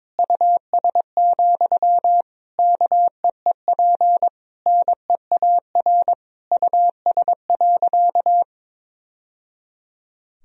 Morse code, heard as a keyed tone, about 22 wpm, 700 Hz. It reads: US, KEEP NEAR US.
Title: Sentences from Top 1000 Words - Review Part 2 - 22wpm